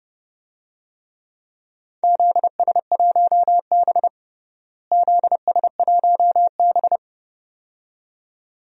Morse code, heard as a keyed tone, about 32 wpm, 700 Hz. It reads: ZS1B ZS1B
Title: DXCC-100 Call Signs - 4 Characters - Repeated 2x - 30wpm